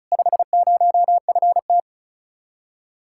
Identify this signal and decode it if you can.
Morse code, heard as a keyed tone, about 46 wpm, 700 Hz. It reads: 50FT